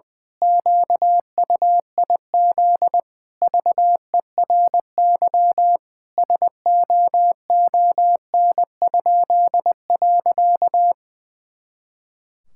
Morse code, heard as a keyed tone, about 20 wpm, 700 Hz. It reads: QUIZ VERY SOON?.
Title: Pangram Challenge v2 - 20wpm